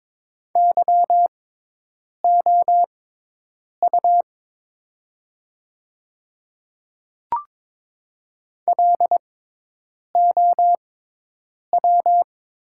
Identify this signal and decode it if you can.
Morse code, heard as a keyed tone, about 22 wpm, spaced as for 8 wpm, 700 Hz. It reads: YOU LOW
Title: 6x Character Spacing - Top 300 Words - 22wpm